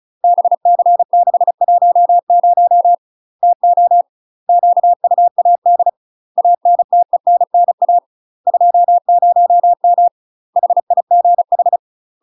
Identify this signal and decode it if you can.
Morse code, heard as a keyed tone, about 35 wpm, 700 Hz. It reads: BC610 TO QUAD ANTENNA 20M HIGH
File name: UK MASTER QSOs TRAINING/35WPM